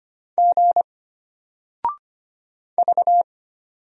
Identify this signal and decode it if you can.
Morse code, heard as a keyed tone, about 25 wpm, 700 Hz. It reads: G V
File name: Single-Letter-Number-25wpm